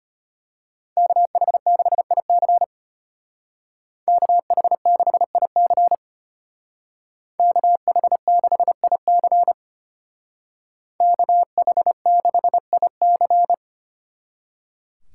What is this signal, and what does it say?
Morse code, heard as a keyed tone, about 31 wpm, 700 Hz. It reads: KH6IC KH6IC KH6IC KH6IC
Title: Speed Racing - USA 2x2 Call Signs 25wpm